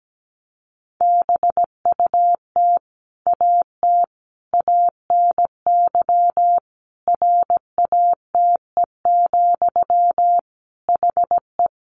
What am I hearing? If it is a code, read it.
Morse code, heard as a keyed tone, about 17 wpm, 700 Hz. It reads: BUT AT ANY RATE, HE
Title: Sentences from Top 700 Words 17wpm